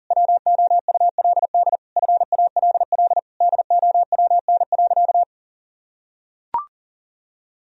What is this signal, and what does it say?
Morse code, heard as a keyed tone, about 40 wpm, 700 Hz. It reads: WOULD FALL DOWN.
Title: Sentences from Top 1000 Words - Review Part 2 - 40wpm